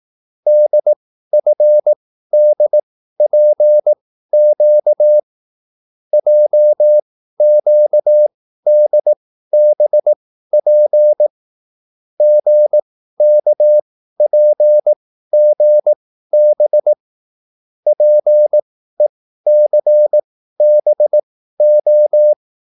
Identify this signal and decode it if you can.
Morse code, heard as a keyed tone, about 18 wpm, spaced as for 13 wpm, 600 Hz. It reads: DFDPQ JQDBP GKPGB PECBO